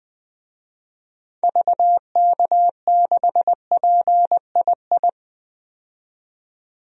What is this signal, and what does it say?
Morse code, heard as a keyed tone, about 21 wpm, 700 Hz. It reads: VK6PII
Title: DXCC-100 Call Signs - 6 Characters - 20wpm